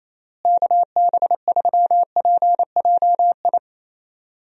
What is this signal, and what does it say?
Morse code, heard as a keyed tone, about 28 wpm, 700 Hz. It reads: KB3PJI
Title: USA Call Signs - 6 Characters - Repeated 2x - 28wpm